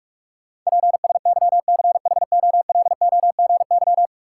Morse code, heard as a keyed tone, about 46 wpm, 700 Hz. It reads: PSYCHOLOGY
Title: Sentences from Top 1400 Words - New Word Drill Alt - 45wpm